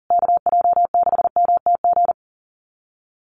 Morse code, heard as a keyed tone, about 40 wpm, 700 Hz. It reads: KJ6MTG
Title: USA Call Signs Repeated 3x 40wpm